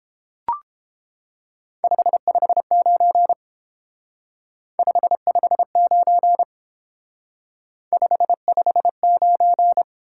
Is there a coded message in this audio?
Yes: Morse code from 1.8 to 9.8 s, heard as a keyed tone, about 29 wpm, 700 Hz. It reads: 559 559 559